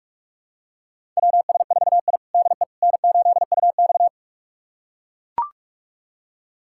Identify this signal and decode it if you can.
Morse code, heard as a keyed tone, about 42 wpm, 700 Hz. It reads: WS4I DE N8UX
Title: USA Callsign DE Callsign - 4 Characters - 45wpm